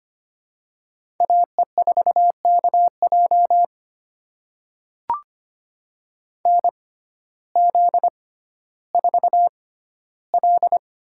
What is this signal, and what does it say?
Morse code, heard as a keyed tone, about 25 wpm, spaced as for 14 wpm, 700 Hz. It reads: AE4KJ NZ4L
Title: USA Call Signs - 6x Character Spacing - 25wpm